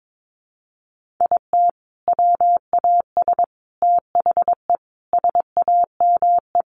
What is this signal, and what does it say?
Morse code, heard as a keyed tone, about 22 wpm, 700 Hz. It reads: IT WAS THE SAME
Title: Sentences from Top 500 Words 22wpm